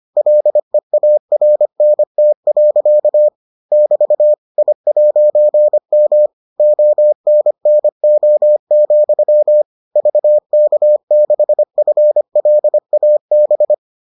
Morse code, heard as a keyed tone, about 25 wpm, 600 Hz. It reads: LEARNT. <BT> I'M ONNO, VK6FLAB